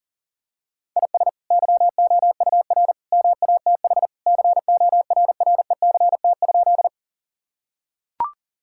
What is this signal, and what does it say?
Morse code, heard as a keyed tone, about 40 wpm, 700 Hz. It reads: IS YOUR MATH CORRECT?
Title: Sentences from Top 1000 Words - Review Part 6 - 40wpm